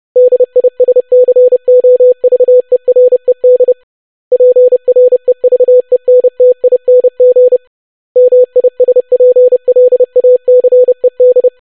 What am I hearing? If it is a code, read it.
Morse code, heard as a keyed tone, about 30 wpm, 500 Hz. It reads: DISCOVERED PREVENTING MISPLACED
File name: longwords-25wpm